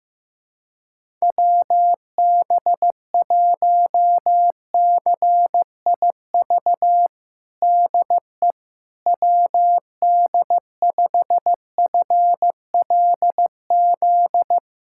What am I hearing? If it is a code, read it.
Morse code, heard as a keyed tone, about 15 wpm, 700 Hz. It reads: WB1CIV DE WD5FLZ
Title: USA Callsign DE Callsign - 6 Characters - 15wpm